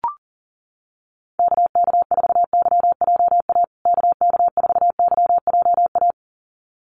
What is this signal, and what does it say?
Morse code, heard as a keyed tone, about 40 wpm, 700 Hz. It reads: KK4YJA KK4YJA
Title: USA Call Signs Repeated 2x 40wpm